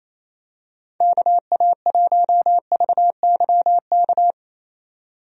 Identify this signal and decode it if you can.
Morse code, heard as a keyed tone, about 28 wpm, 700 Hz. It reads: KA1VYK
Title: USA Call Signs - 6 Characters - 28wpm